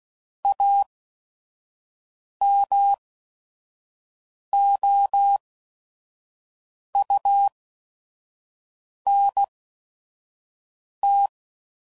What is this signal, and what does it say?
Morse code, heard as a keyed tone, about 16 wpm, spaced as for 5 wpm, 800 Hz. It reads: AMOUNT
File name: CW Code Groups @5WPM / Technical Subjects / XL